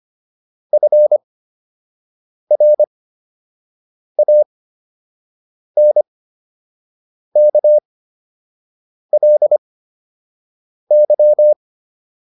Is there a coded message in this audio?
Yes: Morse code from 0.7 to 11.5 s, heard as a keyed tone, about 25 wpm, spaced as for 6 wpm, 600 Hz. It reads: FRANKLY